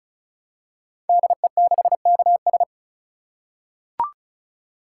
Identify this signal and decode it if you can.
Morse code, heard as a keyed tone, about 36 wpm, 700 Hz. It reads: DE6KS
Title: Code Groups - 5 Letter:Numbers - 35wpm